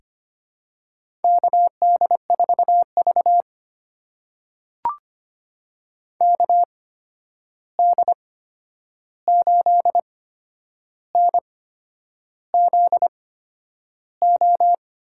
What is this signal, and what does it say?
Morse code, heard as a keyed tone, about 25 wpm, spaced as for 9 wpm, 700 Hz. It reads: KD4V KD8NZO
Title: USA Call Signs - 8x Character Spacing - 25wpm